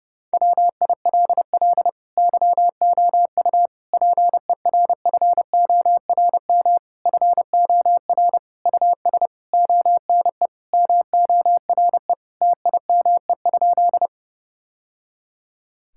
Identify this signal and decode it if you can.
Morse code, heard as a keyed tone, about 32 wpm, 700 Hz. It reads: WILL YOU PERFORM FOR US ONE MORE TIME?
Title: Sentences from Top 1000 Words - Review Part 5 - 30wpm